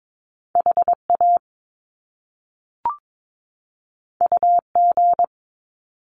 Morse code, heard as a keyed tone, about 22 wpm, 700 Hz. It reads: HA UG
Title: Top Two Letter Combinations in Words 22wpm